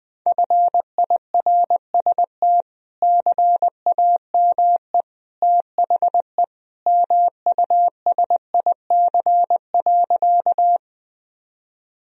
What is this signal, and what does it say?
Morse code, heard as a keyed tone, about 20 wpm, 700 Hz. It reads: FIRST CAME THE MUSIC.
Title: Sentences from Top 1000 Words - Review Part 1 - 20wpm